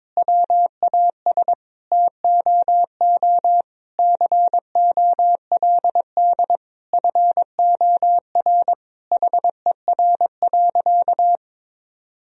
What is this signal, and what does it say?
Morse code, heard as a keyed tone, about 22 wpm, 700 Hz. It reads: WAS TOO COLD FOR HER.
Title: Sentences from Top 1000 Words - Review Part 4 - 22wpm